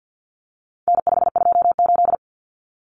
Morse code, heard as a keyed tone, about 50 wpm, 700 Hz. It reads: N528
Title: Common Cut Numbers - 4-Digit 50wpm